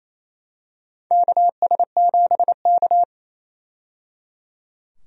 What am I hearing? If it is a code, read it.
Morse code, heard as a keyed tone, about 28 wpm, 700 Hz. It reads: KS7K
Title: USA Call Signs - 4 Characters - 28wpm